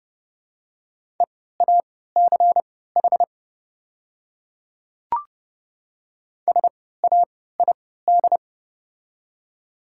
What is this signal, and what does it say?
Morse code, heard as a keyed tone, about 32 wpm, spaced as for 17 wpm, 700 Hz. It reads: EACH SAID
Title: 3x Character Spacing - Top 100 Words - 30wpm